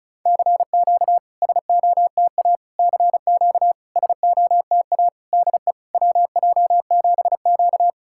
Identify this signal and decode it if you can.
Morse code, heard as a keyed tone, about 34 wpm, 690 Hz. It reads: CQ SOTA CQ SOTA DE WJ7Q
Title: SOTA QSO Practice 35wpm